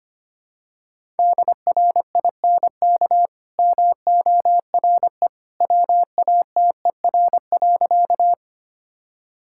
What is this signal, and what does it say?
Morse code, heard as a keyed tone, about 25 wpm, 700 Hz. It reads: DRINK MORE WATER.
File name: Easy Sentences with Hints - 25wpm